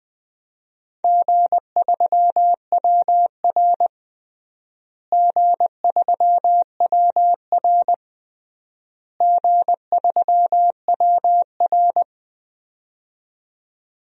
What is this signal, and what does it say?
Morse code, heard as a keyed tone, about 20 wpm, 690 Hz. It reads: G3WR G3WR G3WR
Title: DXCC-100 Call Signs - 4 Characters - Repeated 3x - 20wpm